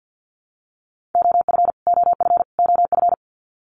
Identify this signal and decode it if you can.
Morse code, heard as a keyed tone, about 50 wpm, 700 Hz. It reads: OF OF OF
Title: Top 300 Words Repeated 50wpm